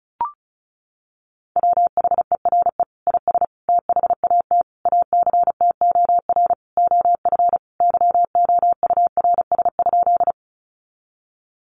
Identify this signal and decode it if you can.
Morse code, heard as a keyed tone, about 34 wpm, 700 Hz. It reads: WHERE IS THAT ACTOR OF YOURS?
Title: Sentences from Top 1500 Words 35wpm